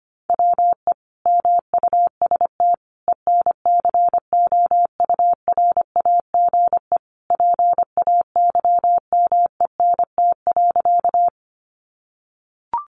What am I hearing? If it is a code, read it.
Morse code, heard as a keyed tone, about 25 wpm, 700 Hz. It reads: WE MUST ENCOURAGE PAYMENT.
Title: Sentences from Top 1400 Words 25wpm